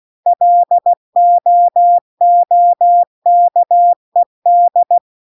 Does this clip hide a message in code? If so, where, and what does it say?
Yes: Morse code from 0.3 to 5.0 s, heard as a keyed tone, about 16 wpm, 700 Hz. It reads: LOOKED